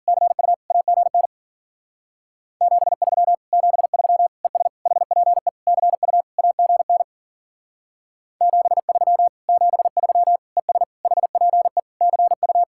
Morse code, heard as a keyed tone, about 50 wpm, 700 Hz. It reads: CU AGN 73 73 ES HPE CU AGN 73 73 ES HPE CU